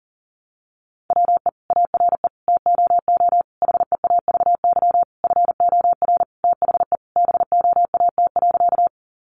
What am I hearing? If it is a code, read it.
Morse code, heard as a keyed tone, about 40 wpm, 700 Hz. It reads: WE ARE TOO HEAVY FOR THE BOAT.
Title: Sentences from Top 1100 Words 40wpm